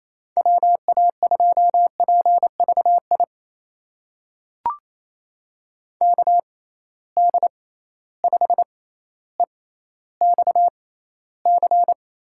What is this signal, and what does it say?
Morse code, heard as a keyed tone, about 28 wpm, spaced as for 14 wpm, 700 Hz. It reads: WA2PVI KD5EXC
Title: USA Call Signs - 6 Characters - 6x Character Spacing - 28wpm